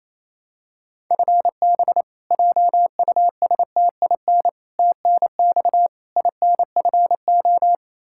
Morse code, heard as a keyed tone, about 28 wpm, 700 Hz. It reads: FB JUSTIN TNX INFO
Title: Practice QSO - Exchange 3 - More Info - 28wpm